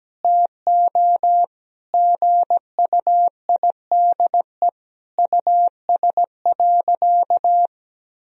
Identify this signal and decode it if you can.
Morse code, heard as a keyed tone, about 17 wpm, 700 Hz. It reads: TO GUIDE US.
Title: Sentences from Top 1000 Words - Review Part 3 - 17wpm